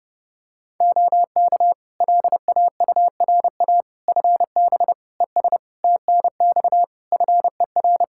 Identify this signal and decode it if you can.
Morse code, heard as a keyed tone, about 32 wpm, 700 Hz. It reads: OK LAURA FB ES TNX FER
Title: K1USN SST QSO Practice 30wpm